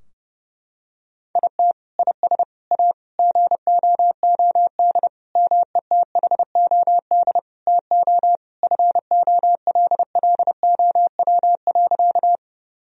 Morse code, heard as a keyed tone, about 30 wpm, 700 Hz. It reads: IT IS A GOOD METHOD TO FOLLOW.
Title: Sentences from Top 1000 Words - Review Part 3 - 30wpm